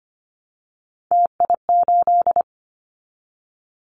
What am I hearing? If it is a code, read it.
Morse code, heard as a keyed tone, about 25 wpm, 700 Hz. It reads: TI8
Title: Callsign Prefixes for Top 100 DXCC - 25wpm